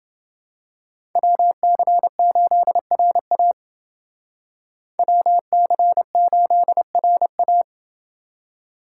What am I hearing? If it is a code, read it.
Morse code, heard as a keyed tone, about 28 wpm, 700 Hz. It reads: WC8RA WC8RA